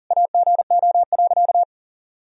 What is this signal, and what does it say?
Morse code, heard as a keyed tone, about 40 wpm, 700 Hz. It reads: AGO.
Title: Sentences from Top 500 Words and Abbreviations 40wpm